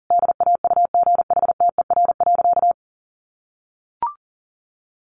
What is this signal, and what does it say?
Morse code, heard as a keyed tone, about 37 wpm, 700 Hz. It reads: DAUGHTER.
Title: Sentences from Top 1500 Words 40wpm